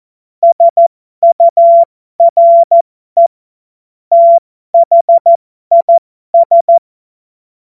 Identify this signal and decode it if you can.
Morse code, heard as a keyed tone, about 14 wpm, spaced as for 12 wpm, 680 Hz. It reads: SURE THIS